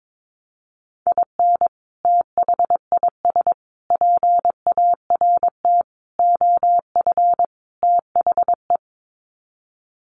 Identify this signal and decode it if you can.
Morse code, heard as a keyed tone, about 22 wpm, 700 Hz. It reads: IN THIS PART OF THE
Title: Sets of 5 Words 22wpm